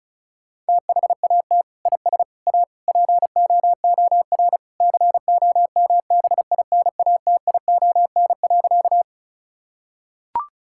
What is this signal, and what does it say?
Morse code, heard as a keyed tone, about 36 wpm, 700 Hz. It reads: THAT IS A POOR COMBINATION.
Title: Sentences from Top 1000 Words - Review Part 6 - 35wpm